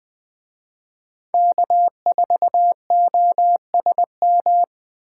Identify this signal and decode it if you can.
Morse code, heard as a keyed tone, about 20 wpm, 700 Hz. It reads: K4OSM